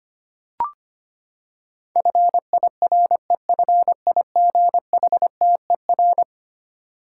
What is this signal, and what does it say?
Morse code, heard as a keyed tone, about 25 wpm, 700 Hz. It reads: FIREFIGHTER
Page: morse-code-ninja-practice-25wpm